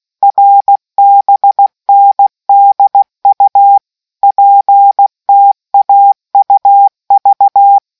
Morse code, heard as a keyed tone, about 16 wpm, 800 Hz. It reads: RBNDU PTAUV